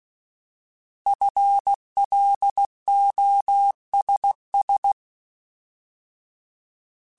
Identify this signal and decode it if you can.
Morse code, heard as a keyed tone, about 16 wpm, 800 Hz. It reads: FLOSS
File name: Alphabetical / Alphabetical mp3s / Fs @15